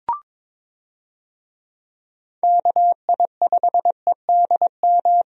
Morse code, heard as a keyed tone, about 22 wpm, 700 Hz. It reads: KI5EDM
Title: USA Call Signs - 6 Characters - 22wpm